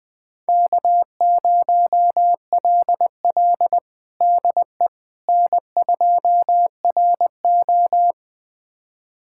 Morse code, heard as a keyed tone, about 20 wpm, 700 Hz. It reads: K0LL DE N2RO
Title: USA Callsign DE Callsign - 4 Characters - 20wpm